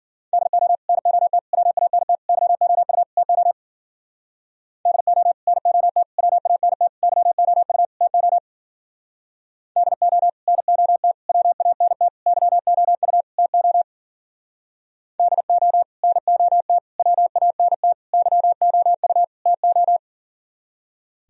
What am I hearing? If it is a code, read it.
Morse code, heard as a keyed tone, about 51 wpm, 700 Hz. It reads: DO NOT WANT YOU TO DO NOT WANT YOU TO DO NOT WANT YOU TO DO NOT WANT YOU TO